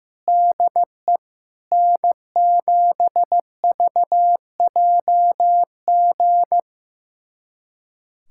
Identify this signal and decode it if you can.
Morse code, heard as a keyed tone, about 15 wpm, 700 Hz. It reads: DE N7VJG
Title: USA Callsign DE Callsign - 5 Characters - 15wpm